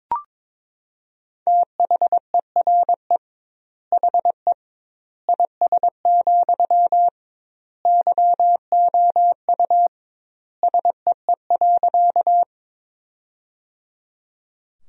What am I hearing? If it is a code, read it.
Morse code, heard as a keyed tone, about 22 wpm, 700 Hz. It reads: THERE HE IS, YOU SEE.
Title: 2x Word-Spacing - Sentences from Top 100 Words - 22wpm